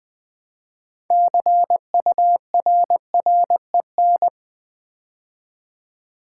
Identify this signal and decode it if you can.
Morse code, heard as a keyed tone, about 20 wpm, 700 Hz. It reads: CURREN
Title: Build A Word - 2 Letters At A Time - 8-Letter Words - 20wpm